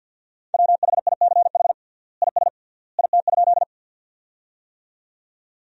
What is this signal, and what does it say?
Morse code, heard as a keyed tone, about 47 wpm, 700 Hz. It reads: WHICH IS IT?
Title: 3x Word-Spacing - Sentences from Top 100 Words - 50wpm